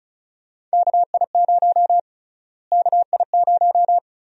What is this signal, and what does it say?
Morse code, heard as a keyed tone, about 36 wpm, 700 Hz. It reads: KI0 KI0
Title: USA Call Sign Prefixes - Repeated 2x - 35wpm